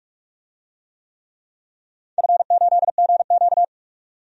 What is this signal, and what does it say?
Morse code, heard as a keyed tone, about 48 wpm, 700 Hz. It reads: F8GQ
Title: DXCC-100 Call Signs - 4 Characters - 45wpm